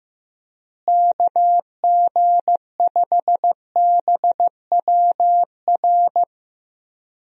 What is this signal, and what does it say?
Morse code, heard as a keyed tone, about 15 wpm, 700 Hz. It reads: KG5BWR
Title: USA Call Signs - 6 Characters - Repeated 3x - 15wpm